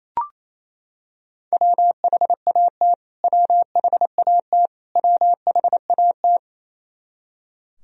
Morse code, heard as a keyed tone, about 28 wpm, 700 Hz. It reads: WHAT WHAT WHAT
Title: Top 100 Words Repeated 28wpm